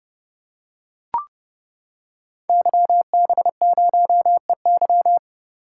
Morse code, heard as a keyed tone, about 30 wpm, 700 Hz. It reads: YB0EY
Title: Code Groups - 5 Letter:Numbers - 30wpm